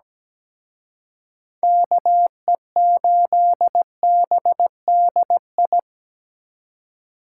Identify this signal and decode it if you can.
Morse code, heard as a keyed tone, about 20 wpm, 700 Hz. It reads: KE8BDI
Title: Speed Racing - USA Call Signs 17wpm